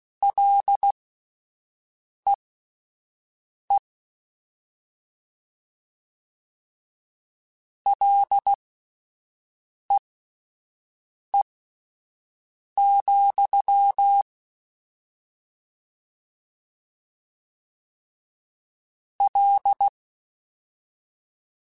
Morse code, heard as a keyed tone, about 16 wpm, spaced as for 5 wpm, 800 Hz. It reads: LEE LEE, L